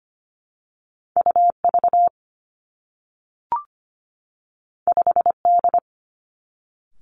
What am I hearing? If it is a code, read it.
Morse code, heard as a keyed tone, about 25 wpm, 700 Hz. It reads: U V 5 D
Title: Two Challenging Letter-Number Combinations 25wpm